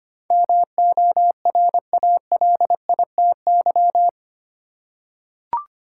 Morse code, heard as a keyed tone, about 25 wpm, 700 Hz. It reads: MORALITY
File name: Eight Letter Words - 25wpm